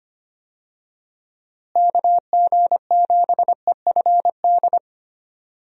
Morse code, heard as a keyed tone, about 25 wpm, 700 Hz. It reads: KG7EFD
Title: USA Call Signs - 6 Characters - Repeated 3x - 25wpm